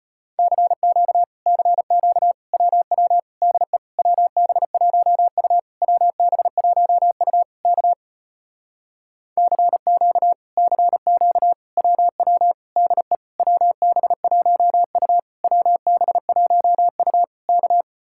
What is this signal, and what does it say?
Morse code, heard as a keyed tone, about 33 wpm, 700 Hz. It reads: CQ CQ WW DE WB1U WB1U K CQ CQ WW DE WB1U WB1U K